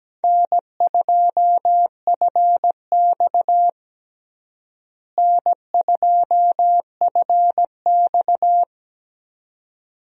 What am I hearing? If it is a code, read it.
Morse code, heard as a keyed tone, about 17 wpm, 700 Hz. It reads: N2FX N2FX
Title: USA Call Signs - 4 Characters - Repeated 2x - 17wpm